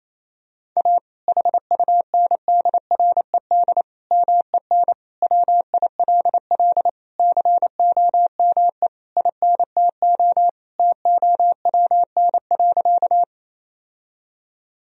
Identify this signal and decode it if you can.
Morse code, heard as a keyed tone, about 28 wpm, 700 Hz. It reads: A HUNDRED MEN WILL COME INTO TOWN.
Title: Sentences from Top 1000 Words - Review Part 2 - 28wpm